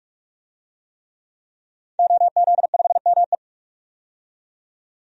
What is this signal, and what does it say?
Morse code, heard as a keyed tone, about 44 wpm, 700 Hz. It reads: OZ5ME
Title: DXCC-100 Call Signs - 5 Characters - 45wpm